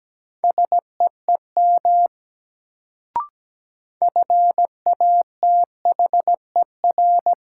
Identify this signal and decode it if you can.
Morse code, heard as a keyed tone, about 17 wpm, 700 Hz. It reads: SEEM FATHER
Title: Sentences from Top 300 Words - New Word Drill - 17wpm